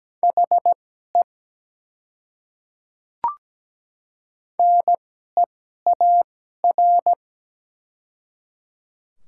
Morse code, heard as a keyed tone, about 17 wpm, spaced as for 12 wpm, 700 Hz. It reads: HE NEAR
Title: 2x Character Spacing - Top 300 Words - 17wpm